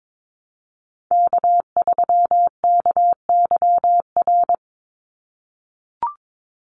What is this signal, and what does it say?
Morse code, heard as a keyed tone, about 22 wpm, 700 Hz. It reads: K3KYR
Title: NA SOTA Chaser Callsigns 22wpm